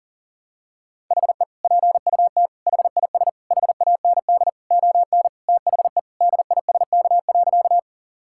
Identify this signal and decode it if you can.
Morse code, heard as a keyed tone, about 40 wpm, 700 Hz. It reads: HE PUT HIS HAND ON THE DISK.
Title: Sentences from Top 1000 Words - Review Part 6 - 40wpm